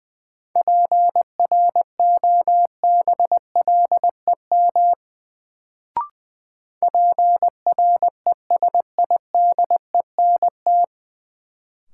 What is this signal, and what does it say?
Morse code, heard as a keyed tone, about 20 wpm, 700 Hz. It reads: PROBLEM PRESIDENT
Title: Sentences from Top 1000 Words - New Word Drill - 20wpm